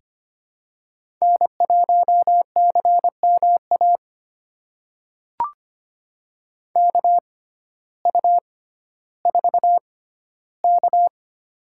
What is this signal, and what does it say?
Morse code, heard as a keyed tone, about 25 wpm, spaced as for 14 wpm, 700 Hz. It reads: N1CMA KU4K
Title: USA Call Signs - 6x Character Spacing - 25wpm